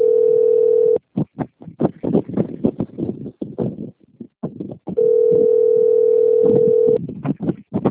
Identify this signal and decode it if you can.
Ringback tone (North America), 440 + 480 Hz.
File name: rg-1600-unknown- - - .43372